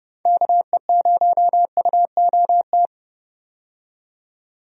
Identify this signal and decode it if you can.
Morse code, heard as a keyed tone, about 30 wpm, 700 Hz. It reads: KE0UOT
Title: USA Call Signs - 6 Characters - 30wpm